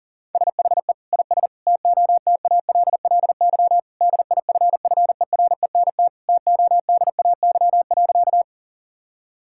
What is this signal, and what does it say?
Morse code, heard as a keyed tone, about 40 wpm, 700 Hz. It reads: SHE IS TOTALLY DIFFERENT TODAY.
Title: Sentences from Top 1600 Words 40wpm